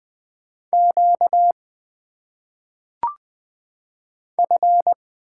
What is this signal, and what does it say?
Morse code, heard as a keyed tone, about 20 wpm, 700 Hz. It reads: Q F
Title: Single Letters - Phonetic Pronunciation 20wpm